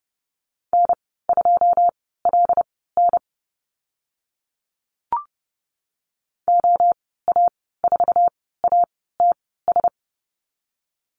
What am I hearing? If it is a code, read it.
Morse code, heard as a keyed tone, about 30 wpm, spaced as for 17 wpm, 700 Hz. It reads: N2LN OA4ATS
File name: Lesson 15g - Number 5 - Call Signs 30wpm